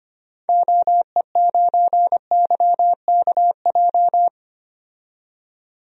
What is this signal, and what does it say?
Morse code, heard as a keyed tone, about 25 wpm, 700 Hz. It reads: OE9YKJ
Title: DXCC-100 Call Signs - 6 Characters - 25wpm